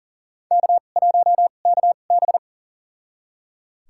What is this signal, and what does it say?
Morse code, heard as a keyed tone, about 37 wpm, spaced as for 28 wpm, 700 Hz. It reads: K1KB
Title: USA Call Signs - 4 Characters - 2x Character Spacing - 40wpm